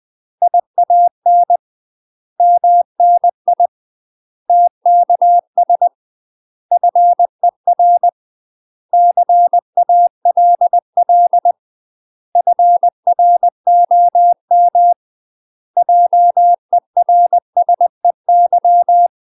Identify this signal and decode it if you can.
Morse code, heard as a keyed tone, about 20 wpm, 700 Hz. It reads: IAN MNI TKS FER CALL FROM JERSEY
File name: UK MASTER QSOs TRAINING / 20WPM / UK MASTER QSOs 0042